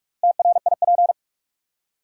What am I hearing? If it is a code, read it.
Morse code, heard as a keyed tone, about 46 wpm, 700 Hz. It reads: TRIP
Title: Four Letter Words - 45wpm